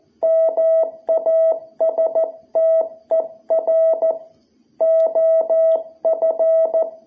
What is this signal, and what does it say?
Morse code, heard as a keyed tone, about 14 wpm, 660 Hz. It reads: MASTER OF
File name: jh8cbh / 2021cw